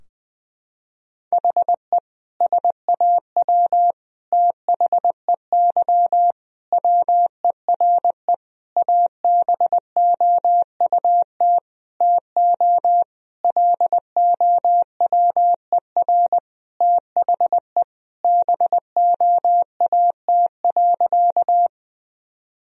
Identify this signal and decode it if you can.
Morse code, heard as a keyed tone, about 20 wpm, 700 Hz. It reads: HE SAW THEY WERE ABOUT TO LOWER THE BOAT.
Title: Sentences from Top 1000 Words - Review Part 4 - 20wpm